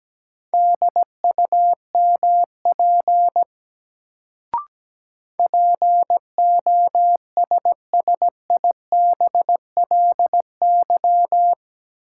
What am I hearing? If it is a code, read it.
Morse code, heard as a keyed tone, about 17 wpm, 700 Hz. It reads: DUMP POSSIBLY
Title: Sentences from Top 1300 Words - New Word Drill - 17wpm